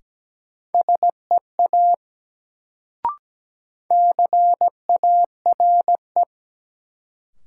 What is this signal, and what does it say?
Morse code, heard as a keyed tone, about 17 wpm, 700 Hz. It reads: SEA CARE
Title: Sentences from Top 300 Words - New Word Drill - 17wpm